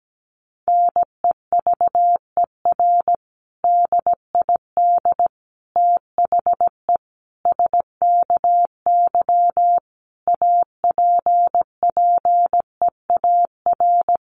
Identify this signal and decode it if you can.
Morse code, heard as a keyed tone, about 17 wpm, 700 Hz. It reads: NEVER DID THE SKY APPEAR
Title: Sentences from Top 1200 Words 17wpm